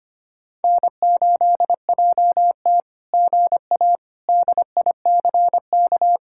Morse code, heard as a keyed tone, about 24 wpm, 700 Hz. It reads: N8JT GA DICK
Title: SOTA QSO Practice 25wpm